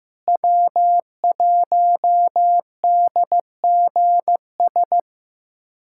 Morse code, heard as a keyed tone, about 15 wpm, 700 Hz. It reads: W1DGS
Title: USA Callsign DE Callsign - 5 Characters - 15wpm